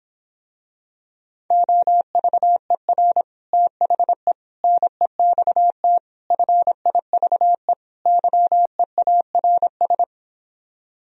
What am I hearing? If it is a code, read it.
Morse code, heard as a keyed tone, about 26 wpm, 700 Hz. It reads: OVER THE NEXT FIVE YEARS